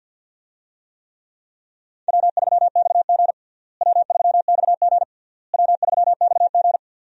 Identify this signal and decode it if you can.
Morse code, heard as a keyed tone, about 47 wpm, 700 Hz. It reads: W3XG W3XG W3XG
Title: USA Call Signs - 4 Characters - Repeated 3x - 50wpm